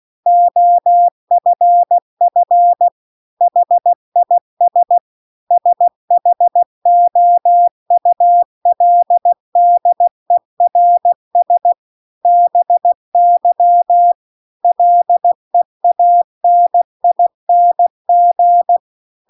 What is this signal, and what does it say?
Morse code, heard as a keyed tone, about 16 wpm, 700 Hz. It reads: OFF HIS SHOULDERS BY LEANING